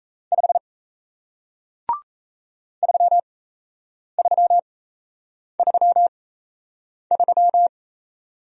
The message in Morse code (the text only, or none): 5 3333